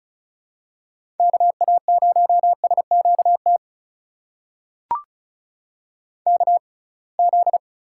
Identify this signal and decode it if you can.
Morse code, heard as a keyed tone, about 36 wpm, spaced as for 27 wpm, 700 Hz. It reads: KA0SQT KZ
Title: USA Call Signs - 6 Characters - 6x Character Spacing - 35wpm